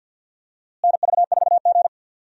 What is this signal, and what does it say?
Morse code, heard as a keyed tone, about 52 wpm, 700 Hz. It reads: NV4G
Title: USA Call Signs - 4 Characters - Repeated 3x - 50wpm